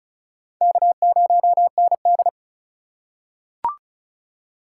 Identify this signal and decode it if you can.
Morse code, heard as a keyed tone, about 36 wpm, 700 Hz. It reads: K0ND
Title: USA Call Signs - 4x Character Spacing - 35wpm